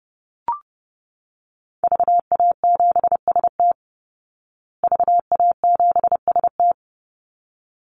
Morse code, heard as a keyed tone, about 30 wpm, 700 Hz. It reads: VA7ST VA7ST
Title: Call Signs Repeated 3x - 30wpm